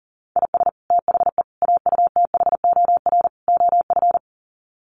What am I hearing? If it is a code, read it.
Morse code, heard as a keyed tone, about 40 wpm, 700 Hz. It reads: IS THE AUTHOR OF
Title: Sets of 4 Words 40wpm